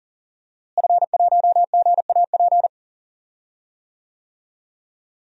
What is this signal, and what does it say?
Morse code, heard as a keyed tone, about 40 wpm, 700 Hz. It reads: F1GAP
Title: DXCC-100 Call Signs - 5 Characters - Repeated 3x - 40wpm